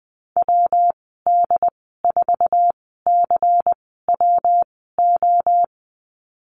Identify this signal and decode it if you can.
Morse code, heard as a keyed tone, about 20 wpm, spaced as for 15 wpm, 700 Hz. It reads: WD4CWO
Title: 122 Lesson 20g - Number 3 - Call Signs 20wpm